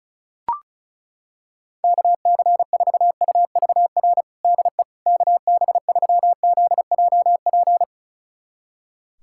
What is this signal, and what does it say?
Morse code, heard as a keyed tone, about 36 wpm, 700 Hz. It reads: KC4UVR DE KB3ZJP
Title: USA Callsign DE Callsign 35wpm